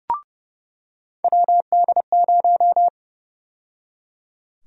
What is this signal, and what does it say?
Morse code, heard as a keyed tone, about 30 wpm, 700 Hz. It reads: WD0
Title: USA Call Sign Prefixes - 30wpm